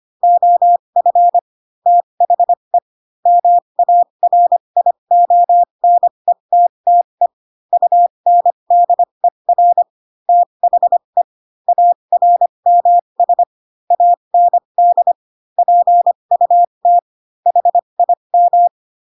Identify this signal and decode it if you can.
Morse code, heard as a keyed tone, about 25 wpm, spaced as for 22 wpm, 700 Hz. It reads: OF THE MARIONETTE UNDER THE ARMS AND PUT HIM
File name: Pinocchio / Pinocchio ch3 18 25 ik1hge